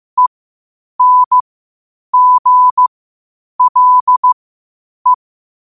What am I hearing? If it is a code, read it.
Morse code, heard as a keyed tone, about 15 wpm, spaced as for 8 wpm, 1000 Hz. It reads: ENGLE
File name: qotdfiles / 8wpm